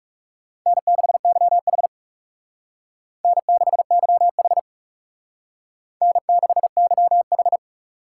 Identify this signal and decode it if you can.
Morse code, heard as a keyed tone, about 38 wpm, 700 Hz. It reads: N6YH N6YH N6YH
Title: Speed Racing - USA 1x2 Call Signs 30wpm